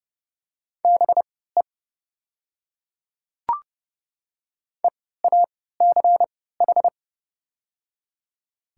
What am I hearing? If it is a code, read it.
Morse code, heard as a keyed tone, about 30 wpm, spaced as for 17 wpm, 700 Hz. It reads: BE EACH
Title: 3x Character Spacing - Top 100 Words - 30wpm